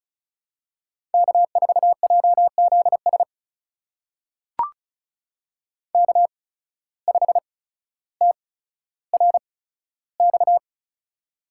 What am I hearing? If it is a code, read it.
Morse code, heard as a keyed tone, about 35 wpm, spaced as for 15 wpm, 690 Hz. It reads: K4JZS K5TRX
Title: USA Call Signs - 5 Characters - 8x Character Spacing - 35wpm